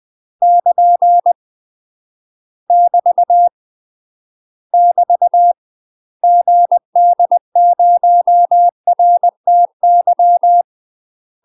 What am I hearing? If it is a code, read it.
Morse code, heard as a keyed tone, about 20 wpm, 700 Hz. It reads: <KN> <BT> <BT> GD0RTY
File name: UK MASTER QSOs TRAINING / 20WPM / UK MASTER QSOs 0042